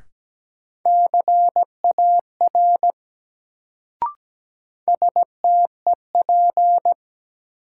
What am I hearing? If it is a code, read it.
Morse code, heard as a keyed tone, about 17 wpm, 700 Hz. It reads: CAR STEP
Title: Sets of 5 Words Encore - New Word Drill - 17wpm